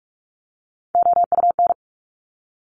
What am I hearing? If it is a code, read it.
Morse code, heard as a keyed tone, about 46 wpm, 700 Hz. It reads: OUN
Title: Top Three Letter Combinations in Words 45wpm